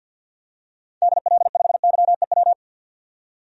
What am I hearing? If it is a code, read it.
Morse code, heard as a keyed tone, about 48 wpm, 700 Hz. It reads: DL5YEW
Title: DXCC-100 Call Signs - 6 Characters - 50wpm